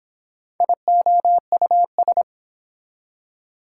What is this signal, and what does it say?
Morse code, heard as a keyed tone, about 30 wpm, 700 Hz. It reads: IOUS